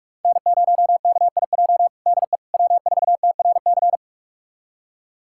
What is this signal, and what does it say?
Morse code, heard as a keyed tone, about 46 wpm, 700 Hz. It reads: N0KIJ DE W4TRC
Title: USA Callsign DE Callsign - 5 Characters - 45wpm